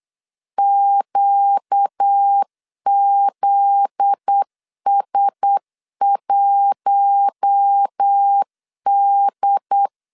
Morse code, heard as a keyed tone, about 8 wpm, 800 Hz. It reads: QZS1D
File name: LEZIONI / 7 GF9Z / 7 GF9Z 8wpm 3213